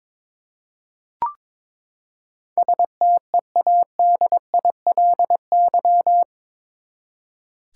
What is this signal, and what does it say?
Morse code, heard as a keyed tone, about 22 wpm, 700 Hz. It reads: STEADILY
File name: practice / 22wpm / Eight Letter Words - 22wpm